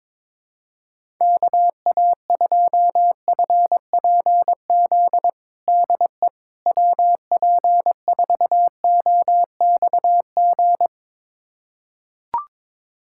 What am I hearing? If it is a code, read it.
Morse code, heard as a keyed tone, about 21 wpm, 700 Hz. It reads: KA2FPZ DE WP4OXG
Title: USA Callsign DE Callsign - 6 Characters - 22wpm